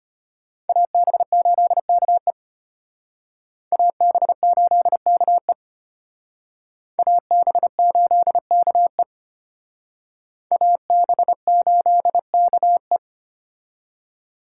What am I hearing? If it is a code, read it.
Morse code, heard as a keyed tone, about 32 wpm, 700 Hz. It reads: AB8KE AB8KE AB8KE AB8KE